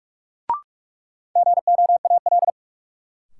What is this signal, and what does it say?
Morse code, heard as a keyed tone, about 44 wpm, 700 Hz. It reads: GOAL
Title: Sentences from Top 1400 Words - New Word Drill Alt - 45wpm